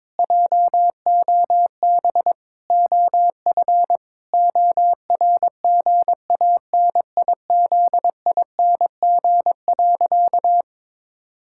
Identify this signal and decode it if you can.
Morse code, heard as a keyed tone, about 22 wpm, 700 Hz. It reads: JOB OF ORGANIZING.
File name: Sentences with 7-Word Phrases - 22wpm